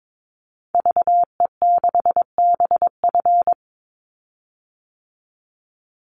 Morse code, heard as a keyed tone, about 22 wpm, 700 Hz. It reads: VE6BF
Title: Call Signs Repeated 2x - 22wpm